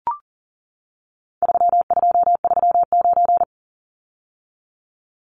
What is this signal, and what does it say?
Morse code, heard as a keyed tone, about 40 wpm, 700 Hz. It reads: 3239
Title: Numbers - 4-Digit 40wpm